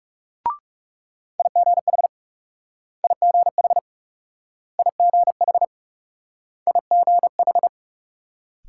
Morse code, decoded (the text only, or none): IGH IGH IGH IGH